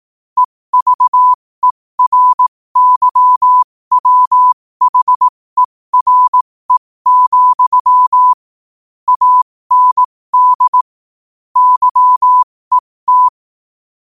Morse code, heard as a keyed tone, about 18 wpm, spaced as for 15 wpm, 1000 Hz. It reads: EVERYWHERE, AND YET
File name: qotdfiles / 15wpm